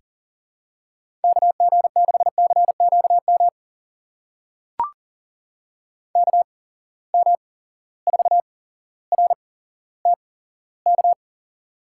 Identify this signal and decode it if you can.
Morse code, heard as a keyed tone, about 43 wpm, spaced as for 17 wpm, 700 Hz. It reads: KG6CQM KM4RTK
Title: USA Call Signs - 6 Characters - 8x Character Spacing - 40wpm